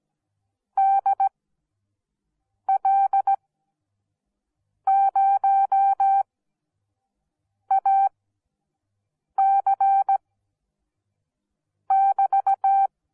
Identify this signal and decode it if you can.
Morse code, heard as a keyed tone, about 17 wpm, spaced as for 5 wpm, 780 Hz. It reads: DL0AC<BT>